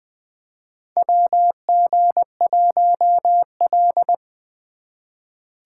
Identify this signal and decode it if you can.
Morse code, heard as a keyed tone, about 21 wpm, 700 Hz. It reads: WG1L
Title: USA Call Signs - 4 Characters - 3x Character Spacing - 20wpm